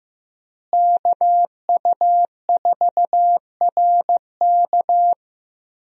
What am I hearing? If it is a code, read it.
Morse code, heard as a keyed tone, about 16 wpm, 700 Hz. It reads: KU4RK
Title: USA Call Signs - 5 Characters - Repeated 2x - 15wpm